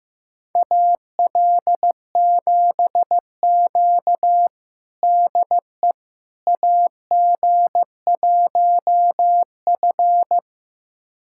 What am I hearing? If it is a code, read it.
Morse code, heard as a keyed tone, about 15 wpm, 700 Hz. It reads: AL7Q DE AG1F
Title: USA Callsign DE Callsign - 4 Characters - 15wpm